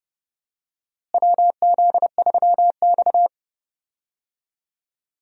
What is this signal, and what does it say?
Morse code, heard as a keyed tone, about 30 wpm, 700 Hz. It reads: WZ3X
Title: USA Call Signs - 4 Characters - Repeated 2x - 30wpm